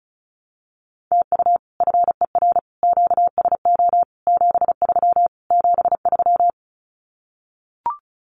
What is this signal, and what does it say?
Morse code, heard as a keyed tone, about 34 wpm, 700 Hz. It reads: TU FER QSO 73 73
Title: Common QSO Phrases Encore 35wpm